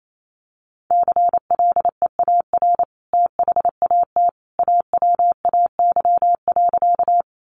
Morse code, heard as a keyed tone, about 28 wpm, 700 Hz. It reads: CLEAR THAT AWAY.
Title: Sentences from Top 600 Words 28wpm